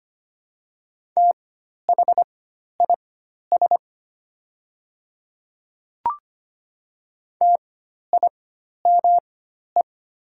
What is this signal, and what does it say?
Morse code, heard as a keyed tone, about 25 wpm, spaced as for 12 wpm, 700 Hz. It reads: THIS TIME
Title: 4x Character Spacing - Top 100 Words - 25wpm